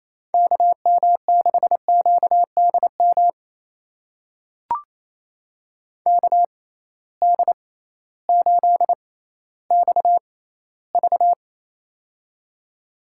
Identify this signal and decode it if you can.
Morse code, heard as a keyed tone, about 28 wpm, spaced as for 15 wpm, 700 Hz. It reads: KM6QDM KD8XV
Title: USA Call Signs - 6x Character Spacing - 28wpm